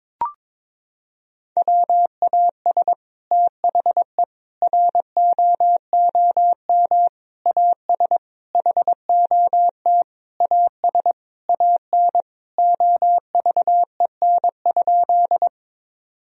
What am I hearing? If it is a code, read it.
Morse code, heard as a keyed tone, about 21 wpm, 700 Hz. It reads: WAS THE ROOM AS HOT AS AN OVEN?
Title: Sentences from Top 1000 Words - Review Part 4 - 22wpm